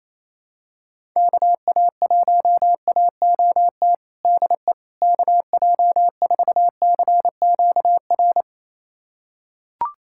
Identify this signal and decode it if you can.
Morse code, heard as a keyed tone, about 27 wpm, 700 Hz. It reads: KA1AOT DE KJ4CQR
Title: USA Callsign DE Callsign 28wpm